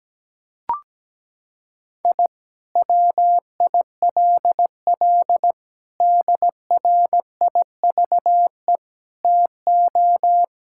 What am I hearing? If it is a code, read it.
Morse code, heard as a keyed tone, about 17 wpm, 700 Hz. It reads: I WILL DRIVE TO
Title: Sentences from Top 1000 Words - Review Part 6 - 17wpm